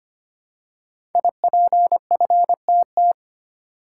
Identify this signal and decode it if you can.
Morse code, heard as a keyed tone, about 25 wpm, 700 Hz. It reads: IPFTT
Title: Code Groups - 5 Letters - 25wpm